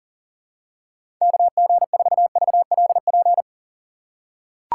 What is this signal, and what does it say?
Morse code, heard as a keyed tone, about 40 wpm, 700 Hz. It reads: KG4VLP
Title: USA Call Signs - 6 Characters - 6x Character Spacing - 40wpm